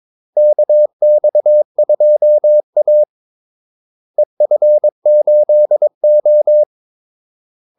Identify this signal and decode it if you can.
Morse code, heard as a keyed tone, about 22 wpm, 600 Hz. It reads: KX2A EF8O
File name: Callsigns-22wpm